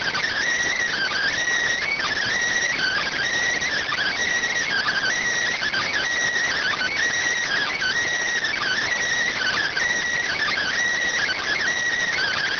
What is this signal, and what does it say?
Emergency Alert System (SAME) tones (header bursts), heard as harsh buzzing bursts. It is heard from 0.7 to 11.7 s.